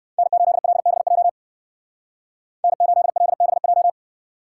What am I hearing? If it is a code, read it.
Morse code, heard as a keyed tone, about 56 wpm, 700 Hz. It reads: N8LBJ N8LBJ
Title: Speed Racing - USA 1x3 Call Signs 45wpm